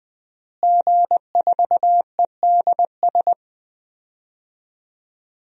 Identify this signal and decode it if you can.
Morse code, heard as a keyed tone, about 20 wpm, 700 Hz. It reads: G4EDS
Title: DXCC-100 Call Signs - 5 Characters - Repeated 2x - 20wpm